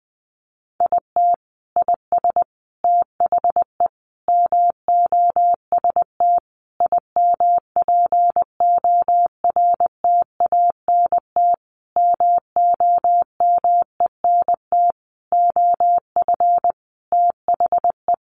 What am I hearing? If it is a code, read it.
Morse code, heard as a keyed tone, about 20 wpm, 700 Hz. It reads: IT IS THE MOST IMPORTANT MOMENT OF THE
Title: Sentences from Top 1500 Words 20wpm